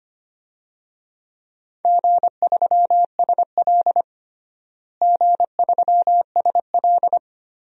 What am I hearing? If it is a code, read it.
Morse code, heard as a keyed tone, about 25 wpm, 700 Hz. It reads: G3SL G3SL
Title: DXCC-100 Call Signs - 4 Characters - Repeated 3x - 25wpm